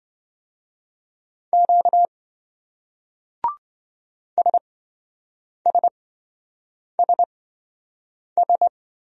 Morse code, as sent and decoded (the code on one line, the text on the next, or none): --.- / ... ... ... ...
Q SSSS